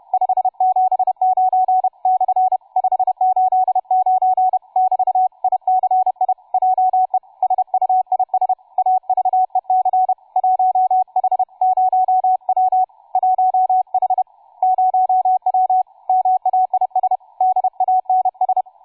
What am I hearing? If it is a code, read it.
Morse code, heard as a keyed tone, about 31 wpm, 750 Hz, with radio noise. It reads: 579 / 589 <BT> ICI JE SUIS AVEC 1H0W 1H0W MAIS DANS
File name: cw forum / tu2 f5ldy 18mhz 16h46